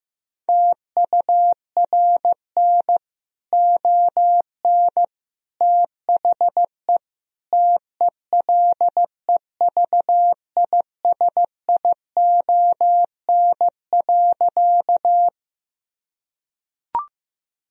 Morse code, heard as a keyed tone, about 16 wpm, 700 Hz. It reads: TURN ON THE TELEVISION.
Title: Sentences from Top 1000 Words - Review Part 4 - 15wpm